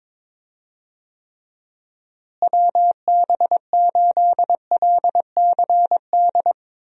Morse code, heard as a keyed tone, about 22 wpm, 700 Hz. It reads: WB8LCD
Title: USA Call Signs - 6 Characters - Repeated 2x - 22wpm